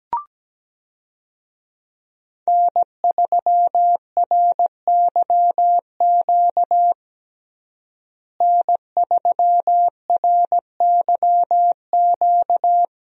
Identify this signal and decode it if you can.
Morse code, heard as a keyed tone, about 17 wpm, 700 Hz. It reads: N3RYQ N3RYQ
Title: USA Call Signs - 5 Characters - Repeated 2x - 17wpm